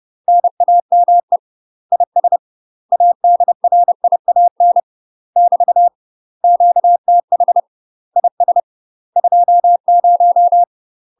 Morse code, heard as a keyed tone, about 30 wpm, 700 Hz. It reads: NAME IS ADRIAN <BT> QTH IS 20
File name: UK MASTER QSOs TRAINING / 30WPM / UK MASTER QSOs 0002